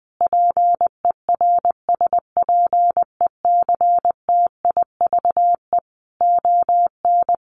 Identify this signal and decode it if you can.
Morse code, heard as a keyed tone, about 20 wpm, 700 Hz. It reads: PERSPECTIVE ON
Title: Sentences from Top 1200 Words 20wpm